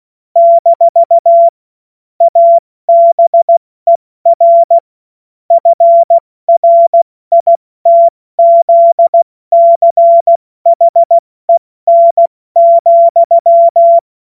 Morse code, heard as a keyed tone, about 16 wpm, 680 Hz. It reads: - ABER FRITZCHEN,
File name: Witz-29-Juli-2025-70-F80